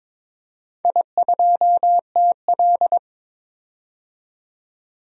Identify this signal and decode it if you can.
Morse code, heard as a keyed tone, about 22 wpm, 700 Hz. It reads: I2TL
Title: DXCC-100 Call Signs - 4 Characters - 22wpm